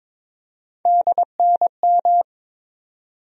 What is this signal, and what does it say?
Morse code, heard as a keyed tone, about 22 wpm, 700 Hz. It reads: DNM
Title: USA Call Sign Postfixes - 3 Letters 22wpm